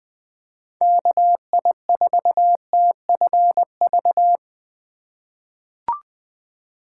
Morse code, heard as a keyed tone, about 20 wpm, 700 Hz. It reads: KI4TFV
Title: USA Call Signs - 6 Characters - 4x Character Spacing - 20wpm